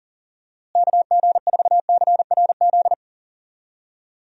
Morse code, heard as a keyed tone, about 40 wpm, 700 Hz. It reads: KG4CRZ
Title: USA Call Signs - 6 Characters - Repeated 2x - 40wpm